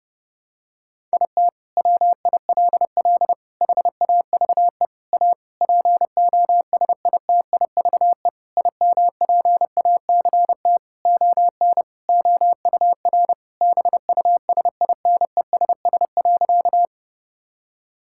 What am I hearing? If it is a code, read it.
Morse code, heard as a keyed tone, about 32 wpm, 700 Hz. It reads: IT WILL HAVE A POSITIVE IMPACT ON OUR BUSINESS.
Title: Sentences from Top 1000 Words - Review Part 6 - 30wpm